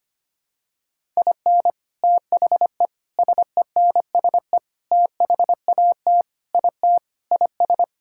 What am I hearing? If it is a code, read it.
Morse code, heard as a keyed tone, about 25 wpm, 700 Hz. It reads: IN THE SENSE THAT IT IS
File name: practice / 25wpm / Sentences with 4-Word Phrases - 25wpm